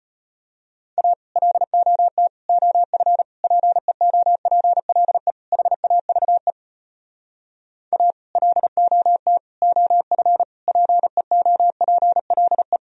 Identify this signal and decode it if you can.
Morse code, heard as a keyed tone, about 33 wpm, 700 Hz. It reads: A LOT OF PEOPLE HAVE A LOT OF PEOPLE